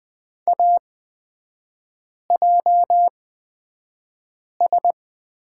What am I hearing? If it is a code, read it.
Morse code, heard as a keyed tone, about 20 wpm, spaced as for 5 wpm, 700 Hz. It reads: AJS
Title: 236 Lesson 38e - Letter q - Three Characters 20wpm